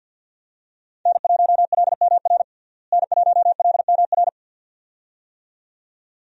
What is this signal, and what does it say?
Morse code, heard as a keyed tone, about 53 wpm, 700 Hz. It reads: N1LMR N1LMR
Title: USA Call Signs - 5 Characters - Repeated 2x - 50wpm